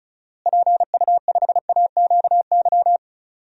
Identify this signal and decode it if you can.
Morse code, heard as a keyed tone, about 36 wpm, 700 Hz. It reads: PU5AQY
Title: DXCC-100 Call Signs - 6 Characters - Repeated 2x - 35wpm